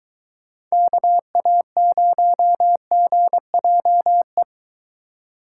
Morse code, heard as a keyed tone, about 22 wpm, 700 Hz. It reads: KA0GJE